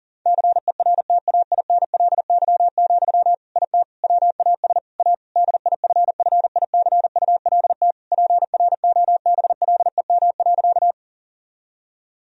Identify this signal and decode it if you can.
Morse code, heard as a keyed tone, about 40 wpm, 700 Hz. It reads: CERTAINLY, IT WAS A DIFFICULT PROBLEM.
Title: Sentences from Top 1000 Words - Review Part 2 - 40wpm